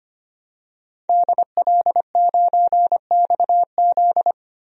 Morse code, heard as a keyed tone, about 25 wpm, 700 Hz. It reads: DL9XZ
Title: DXCC-100 Call Signs - 5 Characters - 25wpm